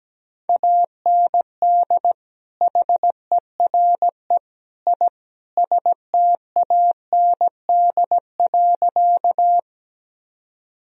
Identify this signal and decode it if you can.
Morse code, heard as a keyed tone, about 17 wpm, 700 Hz. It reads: AND HERE I STAND.
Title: Sentences from Top 1000 Words - Review Part 1 - 17wpm